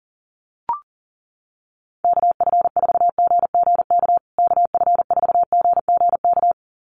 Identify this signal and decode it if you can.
Morse code, heard as a keyed tone, about 40 wpm, 700 Hz. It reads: KF4GGK KF4GGK
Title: USA Call Signs Repeated 2x 40wpm